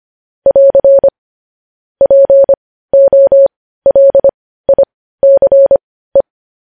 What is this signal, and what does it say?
Morse code, heard as a keyed tone, about 25 wpm, spaced as for 15 wpm, 550 Hz. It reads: <AR> POLICE
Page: news-headlines-in-morse-code-at-15-wpm